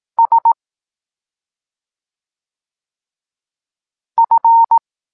Morse code, heard as a keyed tone, about 18 wpm, 920 Hz. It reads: S F